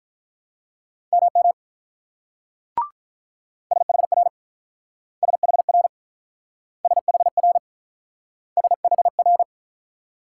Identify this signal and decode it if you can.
Morse code, heard as a keyed tone, about 47 wpm, 690 Hz. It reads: MM SHR SHR SHR SHR